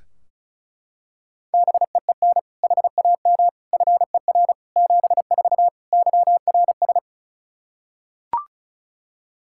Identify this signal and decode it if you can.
Morse code, heard as a keyed tone, about 36 wpm, 700 Hz. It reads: BEEN HAM FER 74 YRS
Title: Practice QSO - Exchange 3 - More Info - 35wpm